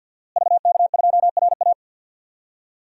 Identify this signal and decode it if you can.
Morse code, heard as a keyed tone, about 51 wpm, 700 Hz. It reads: VK2RA
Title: DXCC-100 Call Signs - 5 Characters - Repeated 3x - 50wpm